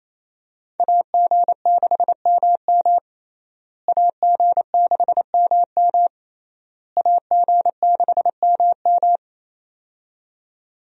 Morse code, heard as a keyed tone, about 28 wpm, 700 Hz. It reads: AG6MM AG6MM AG6MM
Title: USA Call Signs - 5 Characters - Repeated 3x - 28wpm